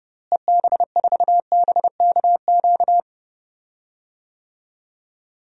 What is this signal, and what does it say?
Morse code, heard as a keyed tone, about 30 wpm, 700 Hz. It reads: EB4BKQ
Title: DXCC-100 Call Signs - 6 Characters - Repeated 2x - 30wpm